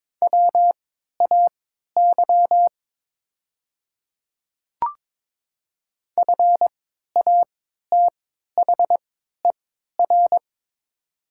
Morse code, heard as a keyed tone, about 22 wpm, spaced as for 12 wpm, 700 Hz. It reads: WAY FATHER
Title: 3x Character Spacing - Top 300 Words - 22wpm